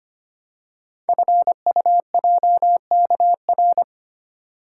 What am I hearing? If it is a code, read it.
Morse code, heard as a keyed tone, about 25 wpm, 700 Hz. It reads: FUJKR
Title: Code Groups - 5 Letters - 25wpm